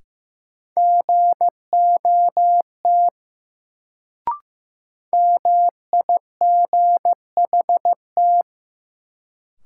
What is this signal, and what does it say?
Morse code, heard as a keyed tone, about 15 wpm, 700 Hz. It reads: GOT MIGHT
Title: Sentences from Top 300 Words - New Word Drill - 15wpm